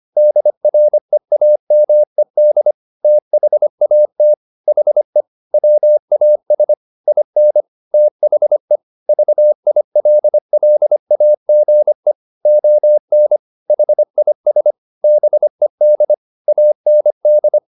Morse code, heard as a keyed tone, about 25 wpm, 600 Hz. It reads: DREAMED THAT HE WAS IN THE VILLAGE ON HIS BED AND